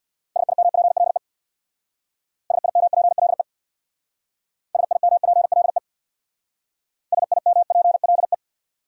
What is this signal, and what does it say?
Morse code, heard as a keyed tone, about 60 wpm, 700 Hz. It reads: SIMPLE SIMPLE SIMPLE SIMPLE
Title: Speed-Racing Top 500 Words 50wpm